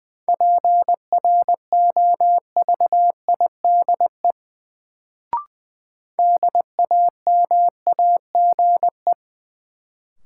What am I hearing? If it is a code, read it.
Morse code, heard as a keyed tone, about 20 wpm, 700 Hz. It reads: PROVIDE DAMAGE
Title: Sentences from Top 1100 Words - New Word Drill Alt - 20wpm